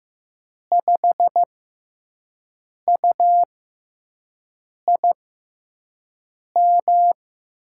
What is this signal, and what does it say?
Morse code, heard as a keyed tone, about 15 wpm, spaced as for 5 wpm, 700 Hz. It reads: 5UIM
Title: USA Call Signs - 6x Character Spacing - 15wpm